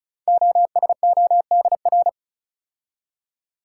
Morse code, heard as a keyed tone, about 36 wpm, 700 Hz. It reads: OSODR
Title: Code Groups - 5 Letters - 35wpm